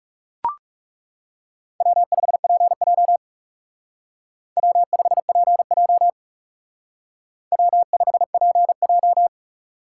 Morse code, heard as a keyed tone, about 37 wpm, 700 Hz. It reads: W5PJ W5PJ W5PJ
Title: Speed Racing - USA 1x2 Call Signs 30wpm